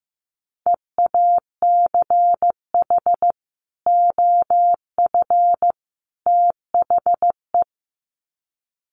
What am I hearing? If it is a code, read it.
Morse code, heard as a keyed tone, about 15 wpm, 700 Hz. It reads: EACH OF THE
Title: Sets of 3 Words 15wpm